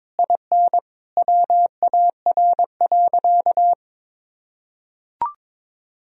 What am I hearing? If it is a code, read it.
Morse code, heard as a keyed tone, about 22 wpm, 700 Hz. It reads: IN WAR.
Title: Sentences from Top 1000 Words - Review Part 2 - 22wpm